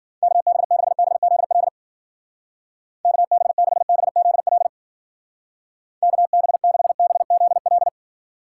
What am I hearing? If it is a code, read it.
Morse code, heard as a keyed tone, about 50 wpm, 700 Hz. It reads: KB6BZL KB6BZL KB6BZL
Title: Speed Racing - USA Call Signs 40wpm